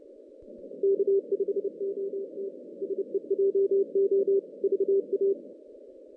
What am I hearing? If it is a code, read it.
Morse code, heard as a keyed tone, about 29 wpm, 390 Hz. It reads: K5OT SEJO VA
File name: 2016 JAN CW NAQP / 2016 Jan naqp k5ot - 0-11